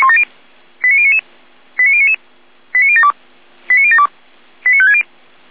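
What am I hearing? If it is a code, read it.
Five-tone selective call (ZVEI), heard as quick runs of five tones, tones of about 70 ms. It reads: 89099 89099 89072 89072 89560
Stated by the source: ZVEI five-tone selective call received by SDR.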